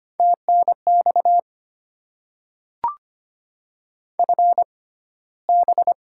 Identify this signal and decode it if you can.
Morse code, heard as a keyed tone, about 25 wpm, spaced as for 15 wpm, 700 Hz. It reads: TNX FB
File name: Top 20 QSO Elements - 6x Character Spacing - 25wpm